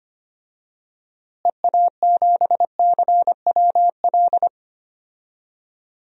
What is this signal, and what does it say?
Morse code, heard as a keyed tone, about 25 wpm, 700 Hz. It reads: EA7CWL
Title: DXCC-100 Call Signs - 6 Characters - Repeated 3x - 25wpm